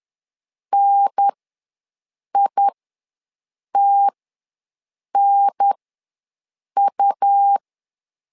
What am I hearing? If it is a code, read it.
Morse code, heard as a keyed tone, about 10 wpm, spaced as for 6 wpm, 800 Hz. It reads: NITNU
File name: LEZIONI / 2 UHDN / 2 UHDN ws 10wpm 2044